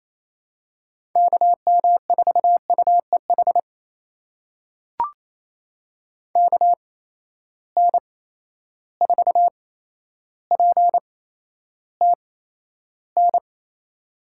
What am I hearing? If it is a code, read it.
Morse code, heard as a keyed tone, about 28 wpm, spaced as for 12 wpm, 700 Hz. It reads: KM4UEH KN4PTN
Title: USA Call Signs - 6 Characters - 8x Character Spacing - 28wpm